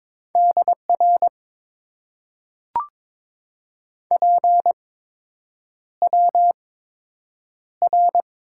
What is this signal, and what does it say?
Morse code, heard as a keyed tone, about 22 wpm, spaced as for 8 wpm, 700 Hz. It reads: DR PWR
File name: Top 90 QSO Elements - 8x Character Spacing - 22wpm